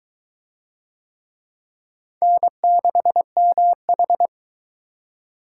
Morse code, heard as a keyed tone, about 20 wpm, 700 Hz. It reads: N6MH